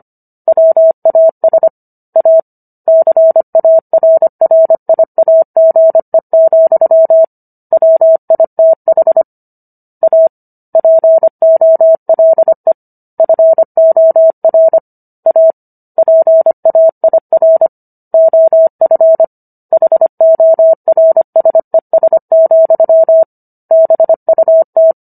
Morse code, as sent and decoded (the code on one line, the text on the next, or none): .-- .- ... / .- / -.-. .- .-. .-. .. .- --. --..-- / .-- .. - .... / .- / .--. --- .-.. / ..-. --- .-. / .- / .--. .- .. .-. / --- ..-. / .... --- .-. ... ... --..-- / -... ..- -
WAS A CARRIAGE, WITH A POLE FOR A PAIR OF HORSES, BUT